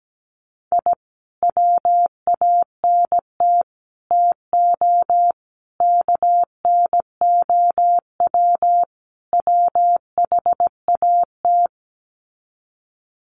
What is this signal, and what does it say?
Morse code, heard as a keyed tone, about 17 wpm, 700 Hz. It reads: I WANT TO KNOW WHAT
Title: Sets of 5 Words Encore2 17wpm